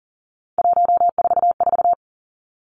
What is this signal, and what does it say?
Morse code, heard as a keyed tone, about 42 wpm, 700 Hz. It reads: 144
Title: Three Challenging Letter-Number Combinations 40wpm